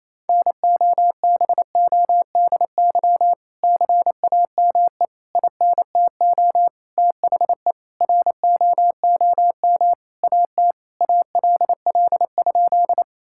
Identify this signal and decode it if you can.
Morse code, heard as a keyed tone, about 28 wpm, 700 Hz. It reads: NOBODY CAME INTO THE ROOM AT ALL?
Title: Sentences from Top 1600 Words 28wpm